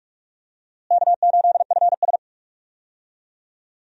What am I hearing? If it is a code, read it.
Morse code, heard as a keyed tone, about 41 wpm, 700 Hz. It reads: K8FS
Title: USA Call Signs - 4 Characters - Repeated 3x - 45wpm